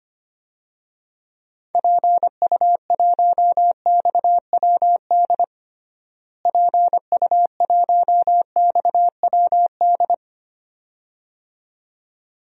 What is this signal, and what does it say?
Morse code, heard as a keyed tone, about 25 wpm, 700 Hz. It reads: PU1XWD PU1XWD
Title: DXCC-100 Call Signs - 6 Characters - Repeated 2x - 25wpm